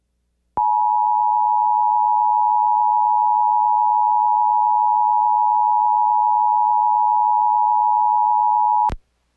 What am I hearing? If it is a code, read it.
Emergency Alert System (SAME) tones (the 853 and 960 Hz attention tone). It is heard from 0.5 to 8.9 s.